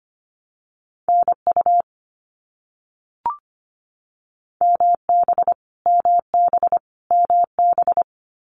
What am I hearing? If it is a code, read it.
Morse code, heard as a keyed tone, about 25 wpm, 700 Hz. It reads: NU MB MB MB
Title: Canadian Provinces and Territories - Repeated 3x - 25wpm